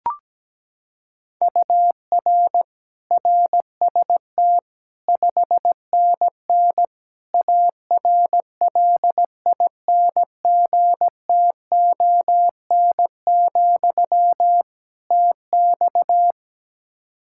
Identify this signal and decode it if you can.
Morse code, heard as a keyed tone, about 17 wpm, 700 Hz. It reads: UR RST 5NN ARLINGTON, TX
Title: Practice QSO - Exchange 1 - Beginning - 17wpm